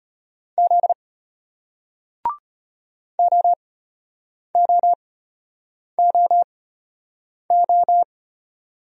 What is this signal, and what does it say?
Morse code, heard as a keyed tone, about 34 wpm, spaced as for 8 wpm, 700 Hz. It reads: Z OOOO